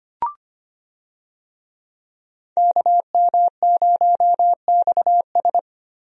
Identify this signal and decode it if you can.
Morse code, heard as a keyed tone, about 25 wpm, 700 Hz. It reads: KM0XS
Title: USA Call Signs - 5 Characters - Repeated 2x - 25wpm